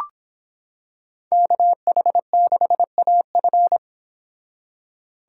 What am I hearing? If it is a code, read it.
Morse code, heard as a keyed tone, about 26 wpm, 700 Hz. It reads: KH6AF